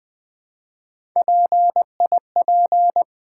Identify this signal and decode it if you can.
Morse code, heard as a keyed tone, about 20 wpm, 700 Hz. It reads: PIP
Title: Build A Word - 1 Letter At A Time - 3 to 5-Letter Words - 20wpm